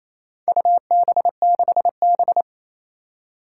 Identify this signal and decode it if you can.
Morse code, heard as a keyed tone, about 27 wpm, 700 Hz. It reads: UB6B
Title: Four Challenging Letter-Number Combinations 28wpm